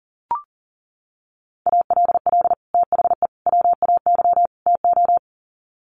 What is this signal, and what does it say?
Morse code, heard as a keyed tone, about 40 wpm, 700 Hz. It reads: ALL THE WAY TO
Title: Sets of 4 Words 40wpm